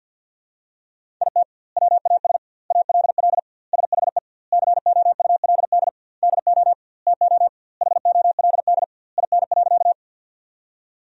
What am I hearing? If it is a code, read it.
Morse code, heard as a keyed tone, about 50 wpm, 700 Hz. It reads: IT WAS ALL SHE COULD DO TO HOLD IN.
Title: 2x Word-Spacing - Sentences from Top 400 Words - 50wpm